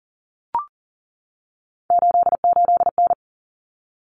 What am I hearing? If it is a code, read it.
Morse code, heard as a keyed tone, about 40 wpm, 700 Hz. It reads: 88N
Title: Common Cut Numbers - 3-Digit 40wpm